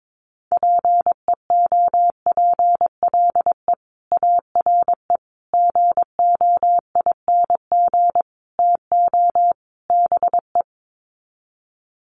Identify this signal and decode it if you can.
Morse code, heard as a keyed tone, about 22 wpm, 700 Hz. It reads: PEOPLE ARE GOING TO BE
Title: Sets of 5 Words Encore 22wpm